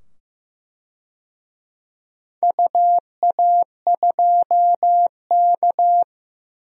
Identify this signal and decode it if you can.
Morse code, heard as a keyed tone, about 15 wpm, 700 Hz. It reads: UA2K
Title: DXCC-100 Call Signs - 4 Characters - 15wpm